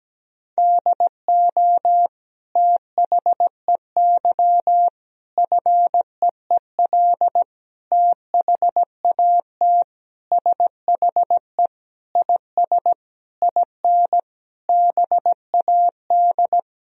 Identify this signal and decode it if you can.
Morse code, heard as a keyed tone, about 17 wpm, 700 Hz. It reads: DO THEY FEEL THAT SHE IS IN BAD
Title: Sentences from Top 1000 Words - Review Part 1 - 17wpm